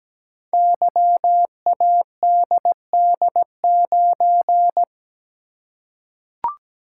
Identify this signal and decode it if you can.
Morse code, heard as a keyed tone, about 17 wpm, 700 Hz. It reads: YADD9
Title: Code Groups - 5 Letter:Numbers - 17wpm